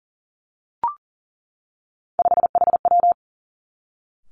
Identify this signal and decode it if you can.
Morse code, heard as a keyed tone, about 38 wpm, 700 Hz. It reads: 5HW
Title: Three Challenging Letter-Number Combinations 40wpm